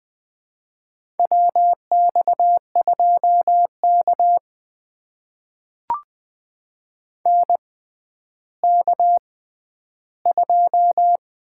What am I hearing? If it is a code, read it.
Morse code, heard as a keyed tone, about 21 wpm, spaced as for 11 wpm, 700 Hz. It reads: WX2K NK2
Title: USA Call Signs - 4 Characters - 6x Character Spacing - 20wpm